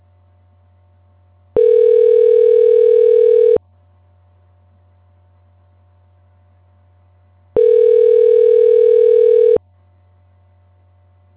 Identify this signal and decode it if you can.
Ringback tone (North America), 440 + 480 Hz.